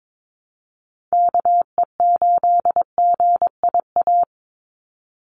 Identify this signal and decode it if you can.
Morse code, heard as a keyed tone, about 22 wpm, 700 Hz. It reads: KE8GIA
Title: USA Call Signs 22wpm